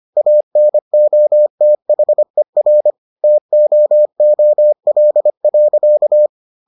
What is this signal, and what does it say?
Morse code, heard as a keyed tone, about 25 wpm, 600 Hz. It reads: ANOTHER TOOL.